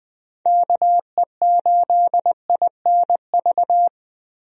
Morse code, heard as a keyed tone, about 20 wpm, 700 Hz. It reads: KE8INV
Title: USA Call Signs - 6 Characters - Repeated 3x - 20wpm